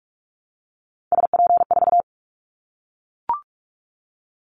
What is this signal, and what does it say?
Morse code, heard as a keyed tone, about 44 wpm, 700 Hz. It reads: SP4
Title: Callsign Prefixes for Top 25 DXCC - Repeated 3x - 45wpm